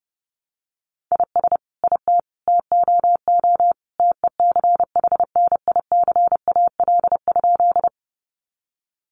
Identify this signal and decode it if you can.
Morse code, heard as a keyed tone, about 30 wpm, 700 Hz. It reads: IS IT TOO TECHNICAL?
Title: Sentences from Top 1100 Words 30wpm